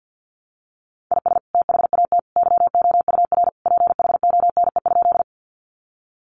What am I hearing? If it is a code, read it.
Morse code, heard as a keyed tone, about 50 wpm, 700 Hz. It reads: IS THAT YOUR PHONE?
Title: Sentences from Top 1100 Words 50wpm